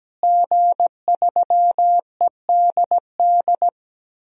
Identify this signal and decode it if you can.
Morse code, heard as a keyed tone, about 17 wpm, 700 Hz. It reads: G3EDD
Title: DXCC-100 Call Signs - 5 Characters - Repeated 3x - 17wpm